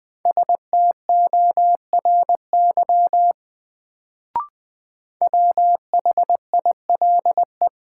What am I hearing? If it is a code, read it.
Morse code, heard as a keyed tone, about 20 wpm, 700 Hz. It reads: STORY WHILE
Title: Sentences from Top 300 Words - New Word Drill - 20wpm